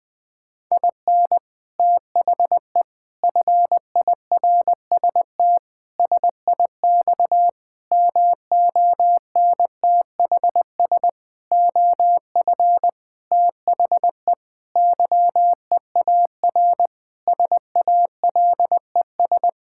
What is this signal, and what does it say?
Morse code, heard as a keyed tone, about 20 wpm, 700 Hz. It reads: IN THE FIRST SIX MONTHS OF THE YEAR SALES
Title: Sentences with 8-Word Phrases - 20wpm